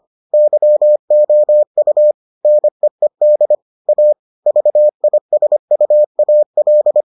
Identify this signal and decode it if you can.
Morse code, heard as a keyed tone, about 25 wpm, 600 Hz. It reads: YOU NEED A VISUAL